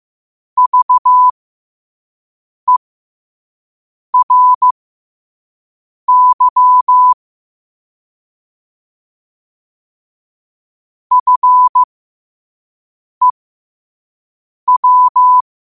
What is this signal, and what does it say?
Morse code, heard as a keyed tone, about 15 wpm, spaced as for 5 wpm, 1000 Hz. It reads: VERY FEW
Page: New quotes every day in morse code at 5 Words per minute.